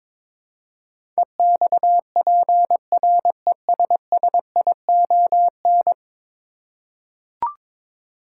Morse code, heard as a keyed tone, about 22 wpm, 700 Hz. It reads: EXPRESSION
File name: practice / 22wpm / 10 Letter Words - First 3 Letters Given - 22wpm